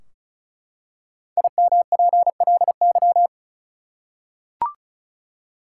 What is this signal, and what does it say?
Morse code, heard as a keyed tone, about 36 wpm, 700 Hz. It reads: IMPLY
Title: Five Letter Words - 35wpm